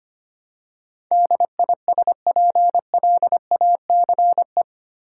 Morse code, heard as a keyed tone, about 25 wpm, 700 Hz. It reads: DISPLACE
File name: Eight Letter Words - 25wpm